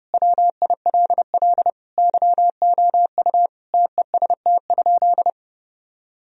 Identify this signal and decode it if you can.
Morse code, heard as a keyed tone, about 32 wpm, 700 Hz. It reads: WILL YOU TEST?
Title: Sentences from Top 1000 Words - Review Part 3 - 30wpm